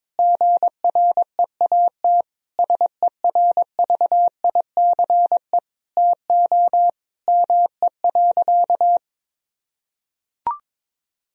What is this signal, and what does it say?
Morse code, heard as a keyed tone, about 22 wpm, 700 Hz. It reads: GREAT SERVICE TO ME.
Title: Sentences from Top 1000 Words - Review Part 4 - 22wpm